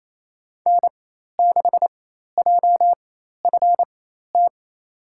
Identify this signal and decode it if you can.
Morse code, heard as a keyed tone, about 28 wpm, spaced as for 13 wpm, 700 Hz. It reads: N6JFT
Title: USA Call Signs - 5 Characters - 4x Character Spacing - 28wpm